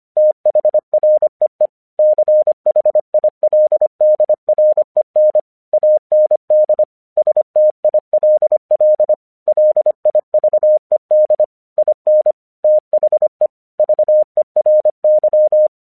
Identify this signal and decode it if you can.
Morse code, heard as a keyed tone, about 25 wpm, 610 Hz. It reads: THREE CHILDREN AND STILL LIVED IN THE VERY
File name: 1 English 25wpm / AQC10